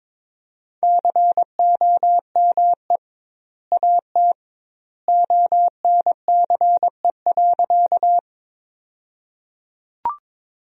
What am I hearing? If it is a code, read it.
Morse code, heard as a keyed tone, about 22 wpm, 700 Hz. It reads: COME AT ONCE.
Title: 2x Word-Spacing - Sentences from Top 300 Words - 22wpm